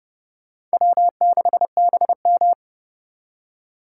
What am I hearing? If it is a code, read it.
Morse code, heard as a keyed tone, about 38 wpm, 700 Hz. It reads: W6BM